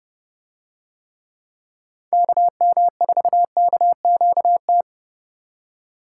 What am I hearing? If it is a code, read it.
Morse code, heard as a keyed tone, about 29 wpm, 700 Hz. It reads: KM4KQT